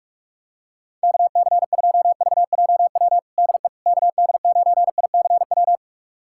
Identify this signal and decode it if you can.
Morse code, heard as a keyed tone, about 46 wpm, 700 Hz. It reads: KC2VJW DE KD9ICW
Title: USA Callsign DE Callsign 45wpm